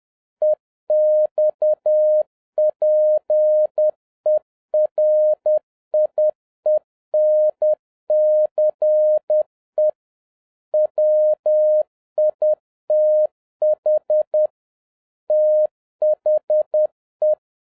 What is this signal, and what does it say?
Morse code, heard as a keyed tone, about 10 wpm, 610 Hz. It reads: EXPERIENCE WITH THE